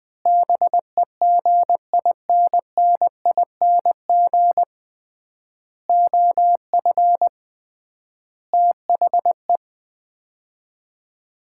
Morse code, heard as a keyed tone, about 20 wpm, 700 Hz. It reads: BEGINNING OF THE
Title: 3x Word-Spacing - Sets of 4 Words - 20wpm